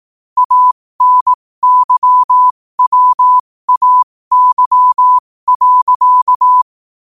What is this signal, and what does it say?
Morse code, heard as a keyed tone, about 18 wpm, spaced as for 15 wpm, 1000 Hz. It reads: ANYWAY.
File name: qotdfiles / 15wpm